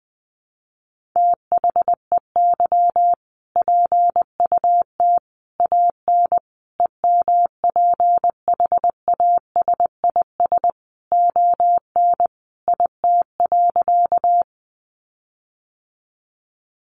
Morse code, heard as a keyed tone, about 20 wpm, 700 Hz. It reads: THEY PUT AN EMPHASIS ON IT.
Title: Sentences from Top 1400 Words 20wpm